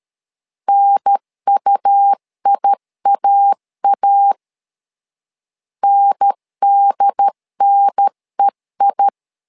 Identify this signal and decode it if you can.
Morse code, heard as a keyed tone, about 13 wpm, 800 Hz. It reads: NUIAA NDNEI
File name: LEZIONI / 2 UHDN / 2 UHDN 12wpm 1456